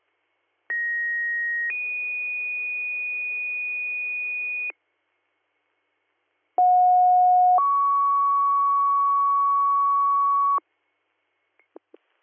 Two-tone page, heard as a one-second tone, then a three-second tone, about 1900 then 2450 Hz, about 730 then 1120 Hz. It is heard from 0.7 to 10.6 s.